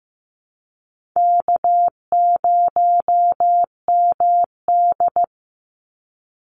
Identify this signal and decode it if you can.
Morse code, heard as a keyed tone, about 15 wpm, 700 Hz. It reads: K0MD
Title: Easy Call Signs Repeated 2x - 15wpm